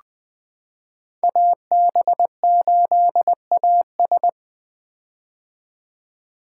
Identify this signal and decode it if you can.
Morse code, heard as a keyed tone, about 20 wpm, 700 Hz. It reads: AB8AS
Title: 2023 POTA Activator Call Signs - 20wpm